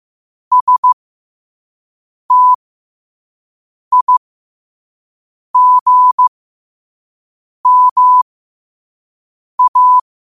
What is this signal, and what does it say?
Morse code, heard as a keyed tone, about 15 wpm, spaced as for 5 wpm, 1000 Hz. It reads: STIGMA
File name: qotdfiles / 5wpm